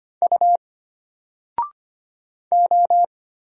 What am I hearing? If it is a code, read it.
Morse code, heard as a keyed tone, about 25 wpm, 700 Hz. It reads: U O
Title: Single Letter-Number 25wpm